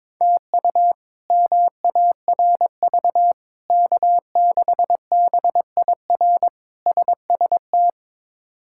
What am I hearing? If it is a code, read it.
Morse code, heard as a keyed tone, about 22 wpm, 700 Hz. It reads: TU MARV K6BIR SST
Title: K1USN SST QSO Practice - 22wpm